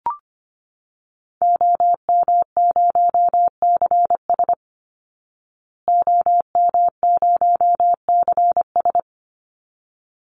Morse code, heard as a keyed tone, about 25 wpm, 700 Hz. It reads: OM0CS OM0CS
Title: Call Signs Repeated 3x - 25wpm